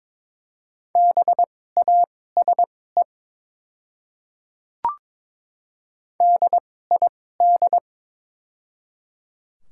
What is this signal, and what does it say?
Morse code, heard as a keyed tone, about 22 wpm, spaced as for 16 wpm, 700 Hz. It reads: BASE DID
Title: 2x Character Spacing - Top 500 Words - 22wpm